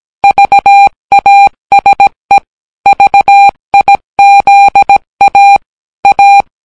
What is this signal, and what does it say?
Morse code, heard as a keyed tone, about 17 wpm, 800 Hz. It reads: VASE VIZA A